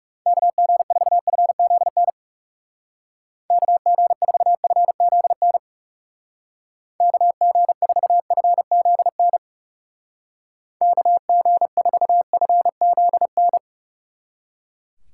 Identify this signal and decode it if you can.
Morse code, heard as a keyed tone, about 37 wpm, 700 Hz. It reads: KG4FZN KG4FZN KG4FZN KG4FZN
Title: Speed Racing - USA 2x3 Call Signs 30wpm